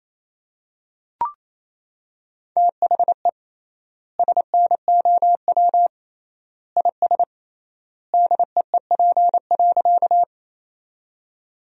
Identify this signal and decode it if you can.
Morse code, heard as a keyed tone, about 28 wpm, 700 Hz. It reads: THE SNOW IS DEEP.
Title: 3x Word-Spacing - Sentences from Top 500 Words - 28wpm